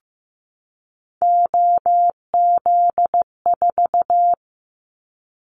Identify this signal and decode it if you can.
Morse code, heard as a keyed tone, about 16 wpm, 700 Hz. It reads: OZ4
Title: Callsign Prefixes for Top 100 DXCC - 15wpm